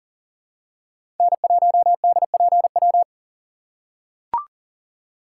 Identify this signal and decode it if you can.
Morse code, heard as a keyed tone, about 40 wpm, 700 Hz. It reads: N1DPW
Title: USA Call Signs - 5 Characters - 6x Character Spacing - 40wpm